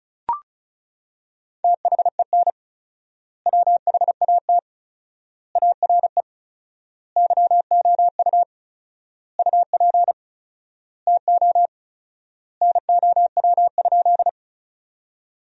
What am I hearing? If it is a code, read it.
Morse code, heard as a keyed tone, about 34 wpm, 700 Hz. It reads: THEN WHAT ARE YOU UP TO NOW?
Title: 4x Word-Spacing - Sentences from Top 100 Words - 35wpm